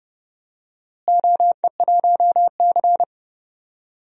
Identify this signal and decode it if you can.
Morse code, heard as a keyed tone, about 30 wpm, 700 Hz. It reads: OE1C
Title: DXCC-100 Call Signs - 4 Characters - Repeated 2x - 30wpm